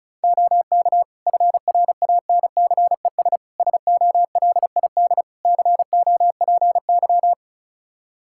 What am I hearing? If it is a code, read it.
Morse code, heard as a keyed tone, about 36 wpm, 700 Hz. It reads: OK FRANCES SOLID COPY
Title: Practice QSO - Exchange 3 - More Info - 35wpm